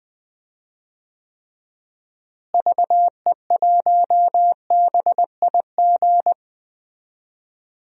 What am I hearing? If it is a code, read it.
Morse code, heard as a keyed tone, about 20 wpm, 700 Hz. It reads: VE1BIG
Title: DXCC-100 Call Signs - 6 Characters - Repeated 3x - 20wpm